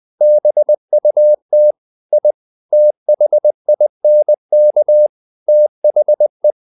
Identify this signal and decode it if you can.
Morse code, heard as a keyed tone, about 20 wpm, 600 Hz. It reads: BUT I THINK THE
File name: the old man and the sea part 2 20wpm